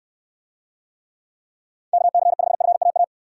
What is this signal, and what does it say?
Morse code, heard as a keyed tone, about 69 wpm, 700 Hz. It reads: KC5FAA